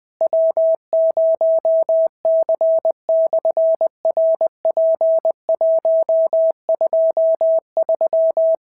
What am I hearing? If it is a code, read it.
Morse code, heard as a keyed tone, about 20 wpm, 650 Hz. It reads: W0C/RP123
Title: SOTA QSO Practice 20wpm